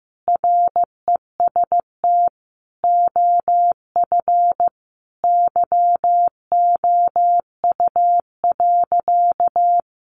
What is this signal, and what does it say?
Morse code, heard as a keyed tone, about 15 wpm, 700 Hz. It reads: REST OF YOU.
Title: Sentences from Top 700 Words 15wpm